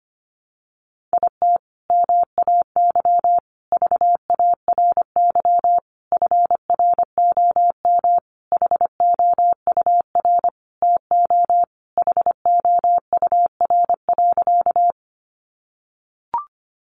Morse code, heard as a keyed tone, about 25 wpm, 700 Hz. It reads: IT MAY VARY FROM HOUR TO HOUR.
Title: Sentences from Top 1200 Words 25wpm